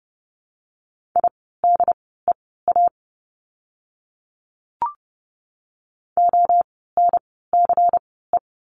Morse code, heard as a keyed tone, about 30 wpm, spaced as for 17 wpm, 700 Hz. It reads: IDEA ONCE
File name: course / 30:10wpm / 108 Lesson 18f - Letter m - Words 30wpm